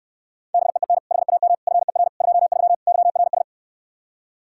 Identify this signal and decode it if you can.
Morse code, heard as a keyed tone, about 69 wpm, 700 Hz. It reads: BEEN HAM FER 23 YRS